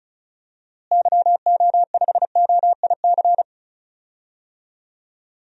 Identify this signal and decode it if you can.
Morse code, heard as a keyed tone, about 34 wpm, 700 Hz. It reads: YO5OIC
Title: DXCC-100 Call Signs - 6 Characters - Repeated 2x - 35wpm